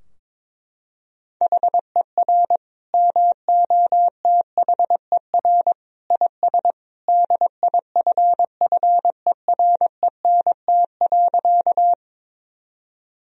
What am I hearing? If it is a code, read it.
Morse code, heard as a keyed tone, about 22 wpm, 700 Hz. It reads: HER MOTHER IS DIFFERENT.
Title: Sentences from Top 1000 Words - Review Part 3 - 22wpm